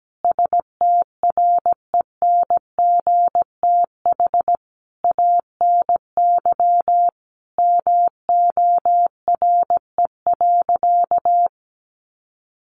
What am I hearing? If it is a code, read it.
Morse code, heard as a keyed tone, about 17 wpm, 700 Hz. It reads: STRENGTH ANY MORE.
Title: Sentences from Top 1300 Words 17wpm